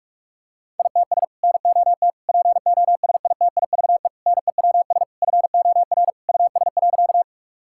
Morse code, heard as a keyed tone, about 41 wpm, 700 Hz. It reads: ITS NOT POSITIVE NEWS FOR US.
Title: Sentences from Top 1000 Words - Review Part 5 - 45wpm